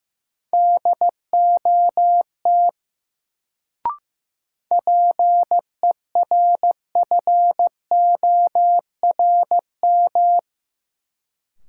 Morse code, heard as a keyed tone, about 15 wpm, 700 Hz. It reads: DOT PERFORM
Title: Sentences from Top 1500 Words - New Word Drill Alt - 15wpm